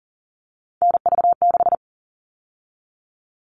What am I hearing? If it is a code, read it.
Morse code, heard as a keyed tone, about 40 wpm, 700 Hz. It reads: NV6
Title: All Cut Numbers - 3-Digit 40wpm